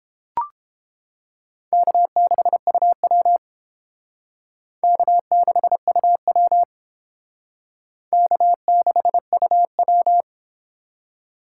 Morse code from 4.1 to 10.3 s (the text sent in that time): K6UW K6UW